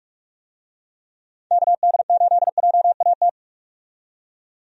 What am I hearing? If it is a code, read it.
Morse code, heard as a keyed tone, about 42 wpm, 700 Hz. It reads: KD8JAT
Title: USA Call Signs - 6 Characters - 45wpm